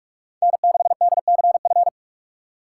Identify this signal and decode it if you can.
Morse code, heard as a keyed tone, about 47 wpm, 700 Hz. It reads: N6DCF
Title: USA Call Signs - 5 Characters - 6x Character Spacing - 45wpm